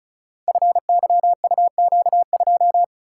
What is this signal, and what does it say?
Morse code, heard as a keyed tone, about 34 wpm, 700 Hz. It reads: FYUQ2
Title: Code Groups - 5 Letter:Numbers - 35wpm